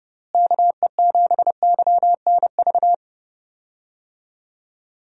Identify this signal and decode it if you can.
Morse code, heard as a keyed tone, about 30 wpm, 700 Hz. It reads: KE7YNV
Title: USA Call Signs - 6 Characters - Repeated 3x - 30wpm